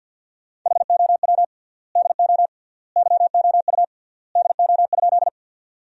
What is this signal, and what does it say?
Morse code, heard as a keyed tone, about 52 wpm, 700 Hz. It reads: HOW DO YOU DO?
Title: 3x Word-Spacing - Sentences from Top 100 Words - 50wpm